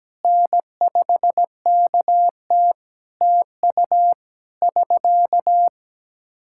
Morse code, heard as a keyed tone, about 17 wpm, 700 Hz. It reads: N5KT TU <SK>
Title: Practice QSO - Exchange 4 - Ending - 17wpm